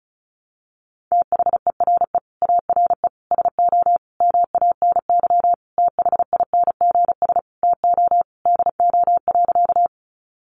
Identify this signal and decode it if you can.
Morse code, heard as a keyed tone, about 34 wpm, 700 Hz. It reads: THERE ARE SO MANY THINGS TO DO.
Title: Additional Easy Phrases 35wpm